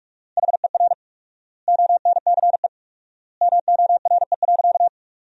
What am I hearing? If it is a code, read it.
Morse code, heard as a keyed tone, about 46 wpm, 700 Hz. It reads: HER ONCE MORE.
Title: 4x Word-Spacing - Sentences from Top 500 Words - 45wpm